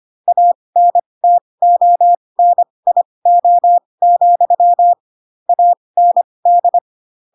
Morse code, heard as a keyed tone, about 25 wpm, spaced as for 20 wpm, 700 Hz. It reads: ANTONIO, AND
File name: Pinocchio / Pinocchio ch2 15 25 ik1hge